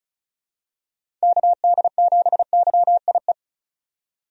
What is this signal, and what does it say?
Morse code, heard as a keyed tone, about 36 wpm, 700 Hz. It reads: KD7YIE
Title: USA Call Signs - 6 Characters - 35wpm